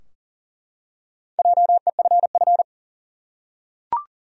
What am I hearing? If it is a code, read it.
Morse code, heard as a keyed tone, about 40 wpm, 700 Hz. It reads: JEFF
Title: USA Common Names - Max 4 Letters - 40wpm